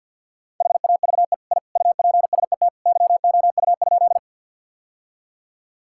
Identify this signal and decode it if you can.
Morse code, heard as a keyed tone, about 51 wpm, 700 Hz. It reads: HAVE I UPSET YOU?
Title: Sentences from Top 1000 Words - Review Part 5 - 50wpm